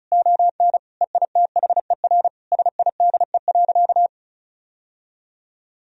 Morse code, heard as a keyed tone, about 36 wpm, 700 Hz. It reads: ON EITHER SIDE.
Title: Sentences from Top 1000 Words - Review Part 4 - 35wpm